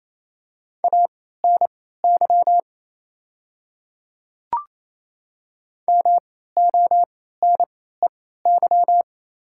Morse code, heard as a keyed tone, about 28 wpm, spaced as for 16 wpm, 690 Hz. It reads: ANY MONEY
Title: 3x Character Spacing - Top 500 Words - 28wpm